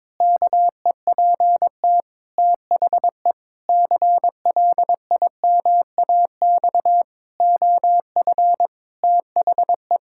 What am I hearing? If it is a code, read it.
Morse code, heard as a keyed tone, about 21 wpm, 700 Hz. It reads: KEPT THE CLIMAX OF THE
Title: Pangram Challenge v2 - 22wpm